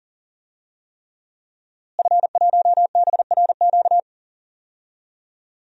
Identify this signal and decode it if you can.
Morse code, heard as a keyed tone, about 40 wpm, 700 Hz. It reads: F1BRQ
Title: DXCC-100 Call Signs - 5 Characters - Repeated 3x - 40wpm